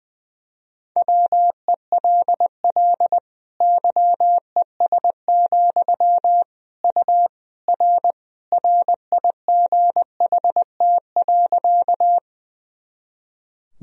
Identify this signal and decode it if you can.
Morse code, heard as a keyed tone, about 21 wpm, 700 Hz. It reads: WELL YES, U R RIGHT.
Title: Sentences from Top 500 Words and Abbreviations 20wpm